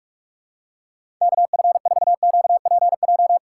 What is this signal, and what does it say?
Morse code, heard as a keyed tone, about 42 wpm, 700 Hz. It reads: KF4QPJ
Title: USA Call Signs - 4x Character Spacing - 45wpm